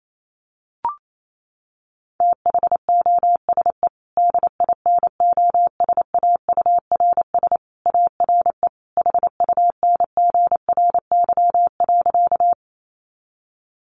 Morse code, heard as a keyed tone, about 28 wpm, 700 Hz. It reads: THOSE DINOSAURS ARE HUNGRY.
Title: A Sound Of Thunder 28wpm